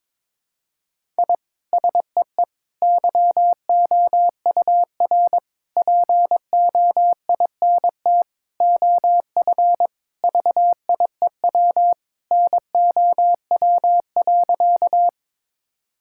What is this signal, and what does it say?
Morse code, heard as a keyed tone, about 22 wpm, 700 Hz. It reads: I SEE YOUR POINT OF VIEW NOW.
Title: Sentences from Top 1000 Words - Review Part 6 - 22wpm